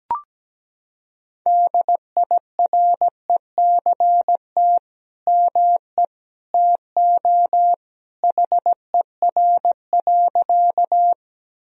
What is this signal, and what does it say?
Morse code, heard as a keyed tone, about 17 wpm, 700 Hz. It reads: DIRECT ME TO HER.
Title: Sentences from Top 1000 Words - Review Part 2 - 17wpm